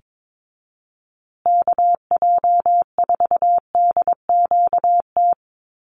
Morse code, heard as a keyed tone, about 22 wpm, 700 Hz. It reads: KJ4DQT
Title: USA Call Signs Repeated 3x 22wpm